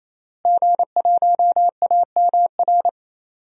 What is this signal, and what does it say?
Morse code, heard as a keyed tone, about 28 wpm, 690 Hz. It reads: G1AMR
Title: DXCC-100 Call Signs - 5 Characters - Repeated 3x - 28wpm